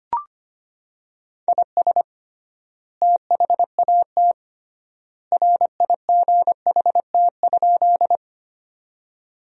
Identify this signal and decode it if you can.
Morse code, heard as a keyed tone, about 25 wpm, 700 Hz. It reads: IS THAT RIGHT?
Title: 3x Word-Spacing - Sentences from Top 200 Words - 25wpm